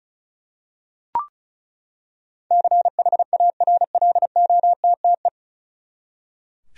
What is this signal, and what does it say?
Morse code, heard as a keyed tone, about 34 wpm, 700 Hz. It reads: CHARLOTTE
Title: USA Common Names - 35wpm